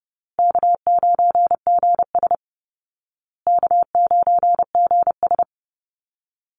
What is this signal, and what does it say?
Morse code, heard as a keyed tone, about 30 wpm, 700 Hz. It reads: K9GS K9GS
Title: Call Signs Repeated 2x - 30wpm